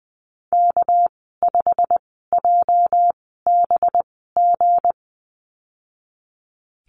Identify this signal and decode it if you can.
Morse code, heard as a keyed tone, about 20 wpm, spaced as for 15 wpm, 700 Hz. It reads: K5JBG
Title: 219 Lesson 35g - Number 0 - Call Signs 20wpm